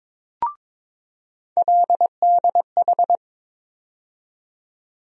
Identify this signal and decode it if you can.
Morse code, heard as a keyed tone, about 23 wpm, 700 Hz. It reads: LDH
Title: USA Call Sign Postfixes - 3 Letters 22wpm